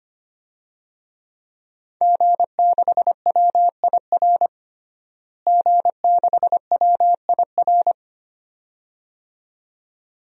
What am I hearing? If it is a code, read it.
Morse code, heard as a keyed tone, about 25 wpm, 700 Hz. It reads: G6WIR G6WIR
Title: DXCC-100 Call Signs - 5 Characters - Repeated 2x - 25wpm